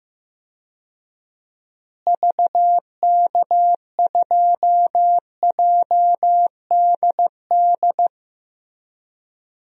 Morse code, heard as a keyed tone, about 16 wpm, 700 Hz. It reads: VK2JDD
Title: DXCC-100 Call Signs - 6 Characters - Repeated 2x - 15wpm